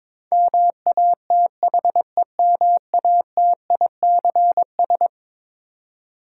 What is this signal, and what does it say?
Morse code, heard as a keyed tone, about 22 wpm, 700 Hz. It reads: MATHEMATICS
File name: practice / 22wpm / Eleven Letter Words - 22wpm